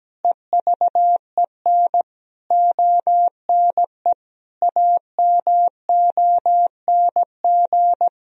Morse code, heard as a keyed tone, about 17 wpm, 700 Hz. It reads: EVEN ONE AMONG
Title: Sentences from Top 1000 Words - Review Part 3 - 17wpm